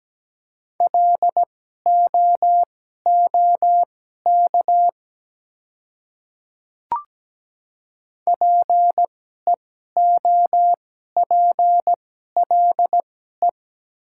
Morse code, heard as a keyed tone, about 17 wpm, spaced as for 12 wpm, 700 Hz. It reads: LOOK PEOPLE
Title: 2x Character Spacing - Top 100 Words - 17wpm